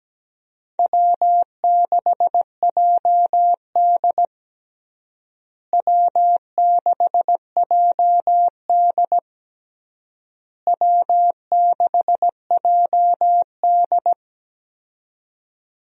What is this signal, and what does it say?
Morse code, heard as a keyed tone, about 17 wpm, 700 Hz. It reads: W6JD W6JD W6JD
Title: USA Call Signs - 4 Characters - Repeated 3x - 17wpm